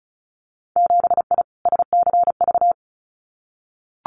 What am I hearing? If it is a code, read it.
Morse code, heard as a keyed tone, about 36 wpm, 700 Hz. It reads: 7I SCV
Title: Winter Field Day Exchanges Encore 35wpm